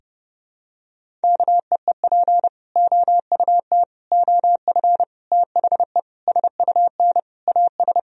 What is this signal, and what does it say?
Morse code, heard as a keyed tone, about 30 wpm, 700 Hz. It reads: KEEP OUT OF THE SUN AS
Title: Sentences from Top 1000 Words - Review Part 1 - 30wpm